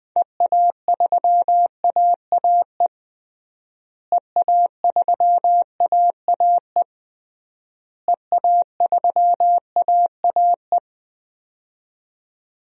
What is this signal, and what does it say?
Morse code, heard as a keyed tone, about 20 wpm, 700 Hz. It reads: EA3AAE EA3AAE EA3AAE
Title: DXCC-100 Call Signs - 6 Characters - Repeated 3x - 20wpm